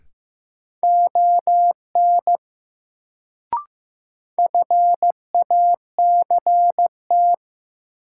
Morse code, heard as a keyed tone, about 15 wpm, 700 Hz. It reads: ON FACT
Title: Sets of 2 Words - New Word Drill - 15wpm